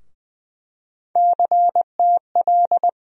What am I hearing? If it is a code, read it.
Morse code, heard as a keyed tone, about 20 wpm, 700 Hz. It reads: CTL
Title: USA Call Sign Postfixes - 3 Letters 20wpm